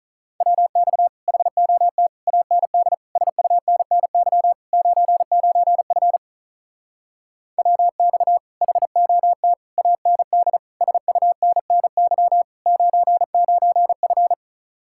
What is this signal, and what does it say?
Morse code, heard as a keyed tone, about 43 wpm, 700 Hz. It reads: WX HOT AND SUNNY 99F WX HOT AND SUNNY 99F
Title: Speed-Racing Common QSO Phrases - 35wpm